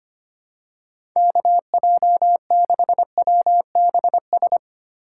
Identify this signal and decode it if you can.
Morse code, heard as a keyed tone, about 25 wpm, 700 Hz. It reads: KJ6WBS
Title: USA Call Signs - 6 Characters - 3x Character Spacing - 25wpm